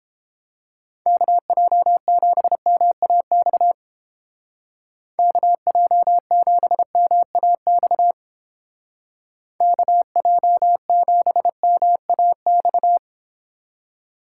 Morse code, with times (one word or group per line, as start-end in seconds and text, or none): 1.1-3.7 KJ7MAX
5.2-8.1 KJ7MAX
9.6-13.0 KJ7MAX